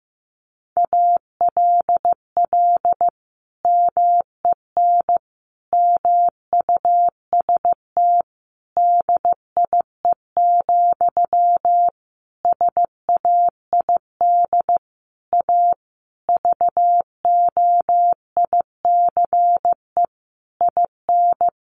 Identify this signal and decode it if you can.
Morse code, heard as a keyed tone, about 15 wpm, 700 Hz. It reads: ALL MEN MUST DIE, SAID A VOICE IN
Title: Sentences from Top 1300 Words 15wpm